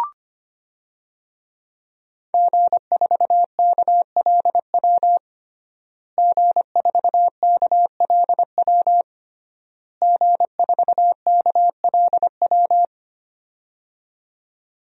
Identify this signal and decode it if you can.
Morse code, heard as a keyed tone, about 25 wpm, 700 Hz. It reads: G4KLW G4KLW G4KLW
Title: DXCC-100 Call Signs - 5 Characters - Repeated 3x - 25wpm